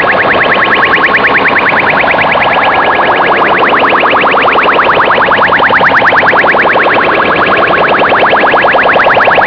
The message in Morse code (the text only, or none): none